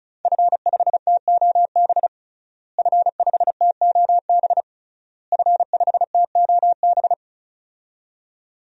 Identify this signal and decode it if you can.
Morse code, heard as a keyed tone, about 36 wpm, 700 Hz. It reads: F5TOB F5TOB F5TOB
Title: DXCC-100 Call Signs - 5 Characters - Repeated 3x - 35wpm